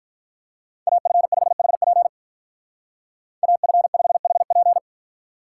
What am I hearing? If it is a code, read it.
Morse code, heard as a keyed tone, about 47 wpm, 700 Hz. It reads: AF5HP AF5HP